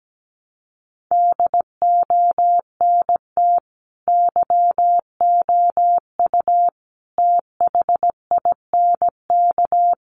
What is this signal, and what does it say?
Morse code, heard as a keyed tone, about 17 wpm, 700 Hz. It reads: DONT YOU THINK
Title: Sentences from Top 300 Words 17wpm